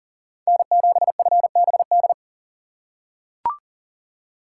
Morse code, heard as a keyed tone, about 40 wpm, 700 Hz. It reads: N7FBD
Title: USA Call Signs - 6x Character Spacing - 40wpm